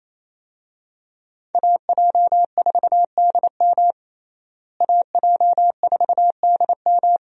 Morse code, heard as a keyed tone, about 28 wpm, 700 Hz. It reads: AJ4DM AJ4DM
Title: USA Call Signs - 5 Characters - Repeated 2x - 28wpm